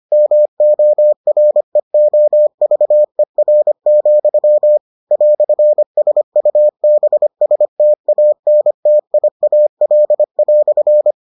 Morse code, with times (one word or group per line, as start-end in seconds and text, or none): 0.1-4.8 MOREOVER,
5.1-11.1 "SUBSTANTIAL"